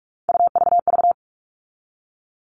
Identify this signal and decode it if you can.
Morse code, heard as a keyed tone, about 44 wpm, 700 Hz. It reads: UVV
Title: All Cut Numbers - 3-Digit 45wpm